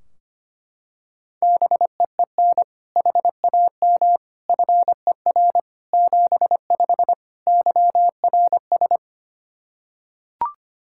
Morse code, heard as a keyed tone, about 25 wpm, 700 Hz. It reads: BEEN HAM FER 75 YRS
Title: Practice QSO - Exchange 3 - More Info - 25wpm